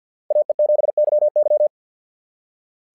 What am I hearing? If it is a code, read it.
Morse code, heard as a keyed tone, about 50 wpm, 690 Hz. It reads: AE7YY
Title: SOTA QSO Practice 50wpm